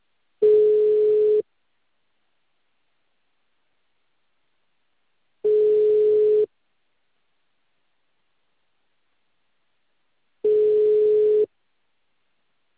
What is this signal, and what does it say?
Ringback tone (Europe, 425 Hz).